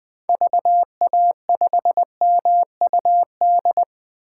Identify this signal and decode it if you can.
Morse code, heard as a keyed tone, about 20 wpm, 700 Hz. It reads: VA5MUD
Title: 2023 POTA Activator Call Signs - 20wpm